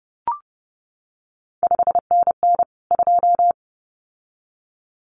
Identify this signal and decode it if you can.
Morse code, heard as a keyed tone, about 30 wpm, 700 Hz. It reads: 5NN 2
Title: CQ WW DX Contest Exchange - 30wpm